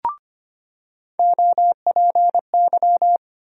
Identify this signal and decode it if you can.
Morse code, heard as a keyed tone, about 25 wpm, 700 Hz. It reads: OPY
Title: USA Call Sign Postfixes - 3 Letters 25wpm